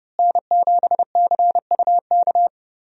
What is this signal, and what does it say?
Morse code, heard as a keyed tone, about 30 wpm, 700 Hz. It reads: N7CUK
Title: USA Call Signs - 5 Characters - Repeated 3x - 30wpm